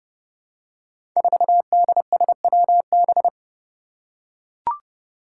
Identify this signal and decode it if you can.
Morse code, heard as a keyed tone, about 30 wpm, 700 Hz. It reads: 4DSWB
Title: Five Challenging Letter-Number Combinations 30wpm